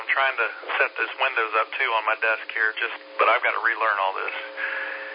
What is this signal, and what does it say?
Emergency Alert System (SAME) tones (header bursts), heard as harsh buzzing bursts, with speech. It is heard from 0.7 to 4.9 s.